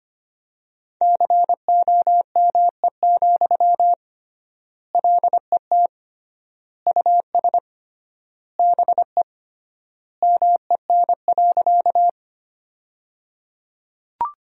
Morse code, heard as a keyed tone, about 25 wpm, 700 Hz. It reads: COME, LET US BE MEN.
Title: 3x Word-Spacing - Sentences from Top 300 Words - 25wpm